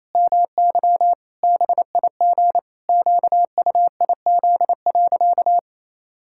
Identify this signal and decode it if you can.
Morse code, heard as a keyed tone, about 28 wpm, 700 Hz. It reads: MY BIG QUIZ.
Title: Pangram Challenge v2 - 28wpm